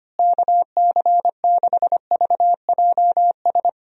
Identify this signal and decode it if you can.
Morse code, heard as a keyed tone, about 25 wpm, 700 Hz. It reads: KC6VJS
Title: USA Call Signs - 6 Characters - Repeated 2x - 25wpm